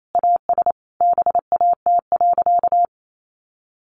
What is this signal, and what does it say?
Morse code, heard as a keyed tone, about 28 wpm, 700 Hz. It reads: AS BAT.
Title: Sentences from Top 1500 Words 28wpm